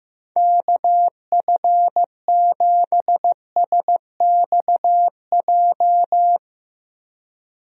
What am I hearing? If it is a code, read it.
Morse code, heard as a keyed tone, about 15 wpm, 700 Hz. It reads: KF7SXJ
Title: USA Call Signs - 6 Characters - Repeated 2x - 15wpm